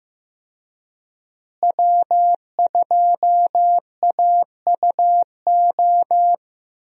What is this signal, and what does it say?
Morse code, heard as a keyed tone, about 15 wpm, 700 Hz. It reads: W2AUO
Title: USA Call Signs - 5 Characters - Repeated 3x - 15wpm